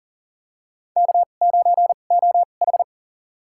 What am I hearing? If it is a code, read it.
Morse code, heard as a keyed tone, about 37 wpm, spaced as for 28 wpm, 700 Hz. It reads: K9OH
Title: USA Call Signs - 2x Character Spacing - 40wpm